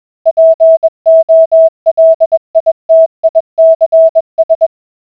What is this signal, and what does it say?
Morse code, heard as a keyed tone, about 21 wpm, 650 Hz. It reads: POLITICS